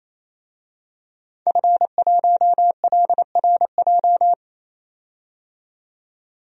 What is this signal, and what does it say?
Morse code, heard as a keyed tone, about 27 wpm, 700 Hz. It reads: F1LRJ
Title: DXCC-100 Call Signs - 5 Characters - 28wpm